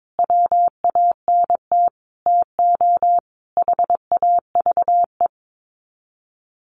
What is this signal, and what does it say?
Morse code, heard as a keyed tone, about 22 wpm, 700 Hz. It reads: WANT TO HAVE
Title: Sets of 5 Words 22wpm